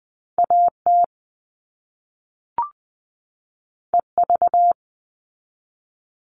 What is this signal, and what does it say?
Morse code, heard as a keyed tone, about 20 wpm, 700 Hz. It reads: AT EV
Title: All Cut Numbers - 2-Digit 20wpm